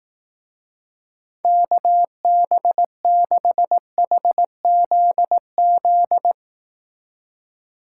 Morse code, heard as a keyed tone, about 18 wpm, 700 Hz. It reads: KB6HZZ
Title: Speed Racing - USA 2x3 Call Signs 15wpm